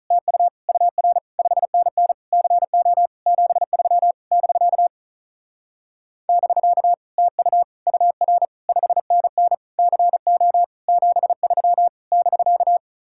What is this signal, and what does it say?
Morse code, heard as a keyed tone, about 42 wpm, 700 Hz. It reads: TU UR 5NN CO 73 <BK> <BK> TU UR 5NN CO 73 <BK>
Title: Speed-Racing Common QSO Phrases - 35wpm